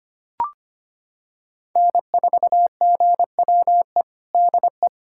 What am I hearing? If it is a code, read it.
Morse code, heard as a keyed tone, about 25 wpm, 700 Hz. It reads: N4GWE DE
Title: USA Callsign DE Callsign 25wpm